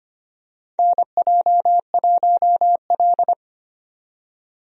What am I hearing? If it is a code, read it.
Morse code, heard as a keyed tone, about 25 wpm, 700 Hz. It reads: NJ1L
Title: USA Call Signs - 4 Characters - 25wpm